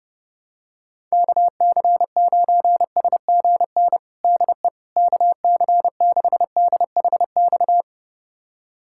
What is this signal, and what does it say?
Morse code, heard as a keyed tone, about 32 wpm, 700 Hz. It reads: KC9SGN DE KC6DHX
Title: USA Callsign DE Callsign 30wpm